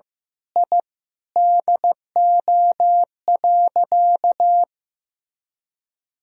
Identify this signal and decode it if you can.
Morse code, heard as a keyed tone, about 15 wpm, 700 Hz. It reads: I DO.
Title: Sentences from Top 1000 Words - Review Part 3 - 15wpm